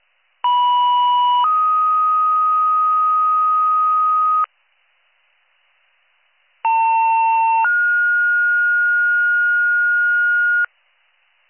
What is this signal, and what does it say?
Two-tone page, heard as a one-second tone, then a three-second tone, about 980 then 1250 Hz, about 900 then 1430 Hz. It is heard from 0.4 to 10.7 s.